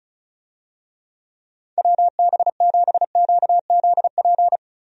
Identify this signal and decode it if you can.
Morse code, heard as a keyed tone, about 34 wpm, 700 Hz. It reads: WB7QZP
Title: USA Call Signs - 6 Characters - Repeated 2x - 35wpm